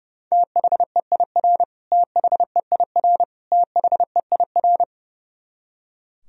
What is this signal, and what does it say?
Morse code, heard as a keyed tone, about 31 wpm, 700 Hz. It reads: THEIR THEIR THEIR
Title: Top 100 Words Repeated 30wpm